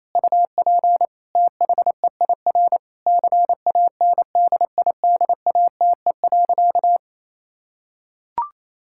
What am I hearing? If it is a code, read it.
Morse code, heard as a keyed tone, about 28 wpm, 700 Hz. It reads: UP THEIR CANDIDATE.
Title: Sentences from Top 1600 Words 28wpm